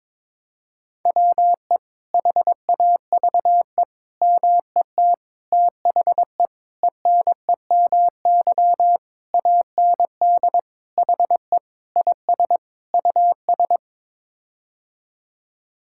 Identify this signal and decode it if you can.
Morse code, heard as a keyed tone, about 22 wpm, 700 Hz. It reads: WE HAVE MET THE ENEMY AND HE IS US
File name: Sets of 9 Words - 22wpm